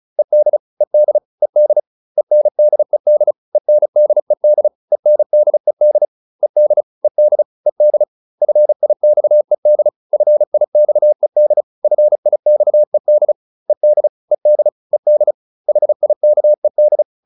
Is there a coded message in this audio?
Yes: Morse code from 0.2 to 17.0 s, heard as a keyed tone, about 34 wpm, 600 Hz. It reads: ED ED ED ENDED ENDED ENDED ED ED ED FIXED FIXED FIXED ED ED ED HIKED